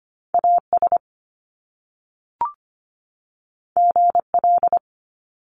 Morse code, heard as a keyed tone, about 25 wpm, 700 Hz. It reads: AS GL
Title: Top Two Letter Combinations in Words 25wpm